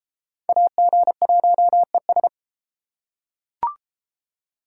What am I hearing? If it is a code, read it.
Morse code, heard as a keyed tone, about 30 wpm, 700 Hz. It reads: AG1ES